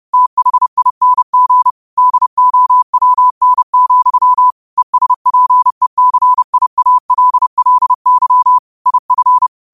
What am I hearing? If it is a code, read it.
Morse code, heard as a keyed tone, about 30 wpm, 1000 Hz. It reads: THING DOWN, ESPECIALLY IF
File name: qotdfiles / 30wpm